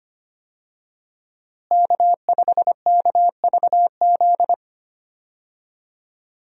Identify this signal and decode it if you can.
Morse code, heard as a keyed tone, about 25 wpm, 700 Hz. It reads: K5KVZ
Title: USA Call Signs - 5 Characters - Repeated 3x - 25wpm